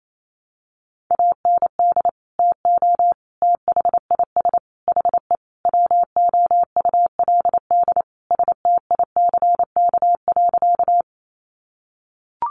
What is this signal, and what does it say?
Morse code, heard as a keyed tone, about 28 wpm, 700 Hz. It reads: AND TO THIS HE WOULD STICK.
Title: Sentences from Top 1200 Words 28wpm